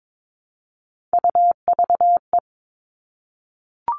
Morse code, heard as a keyed tone, about 22 wpm, 700 Hz. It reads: UVE
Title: All Cut Numbers - 3-Digit 22wpm